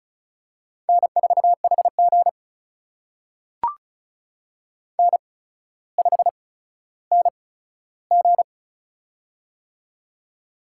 Morse code, heard as a keyed tone, about 36 wpm, spaced as for 15 wpm, 700 Hz. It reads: N4HG N5NG
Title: USA Call Signs - 4 Characters - 8x Character Spacing - 35wpm